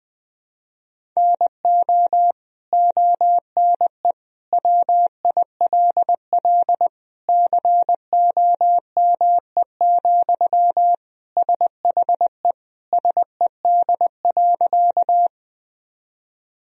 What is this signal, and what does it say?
Morse code, heard as a keyed tone, about 21 wpm, 700 Hz. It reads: NO ONE WILL COME, SHE SED.
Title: Sentences from Top 100 Words and Abbreviations 20wpm